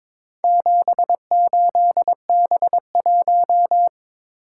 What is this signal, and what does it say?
Morse code, heard as a keyed tone, about 22 wpm, 700 Hz. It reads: 78B1
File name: practice / 22wpm / Four Letter-Number Combinations - 22wpm